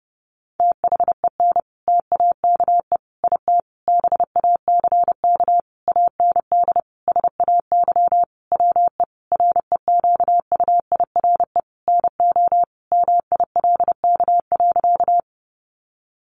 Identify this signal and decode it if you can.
Morse code, heard as a keyed tone, about 30 wpm, 700 Hz. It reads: THEN TAKE IT BACK AND SAY WE REQUIRE NO MILK.
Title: Sentences from Top 1500 Words 30wpm